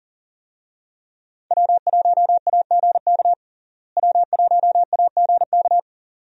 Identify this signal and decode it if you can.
Morse code, heard as a keyed tone, about 40 wpm, 700 Hz. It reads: W1AGK W1AGK
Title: USA Call Signs - 5 Characters - Repeated 2x - 40wpm